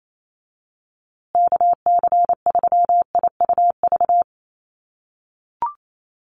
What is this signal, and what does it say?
Morse code, heard as a keyed tone, about 28 wpm, 700 Hz. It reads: KC3IUV
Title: USA Call Signs Repeated 2x 28wpm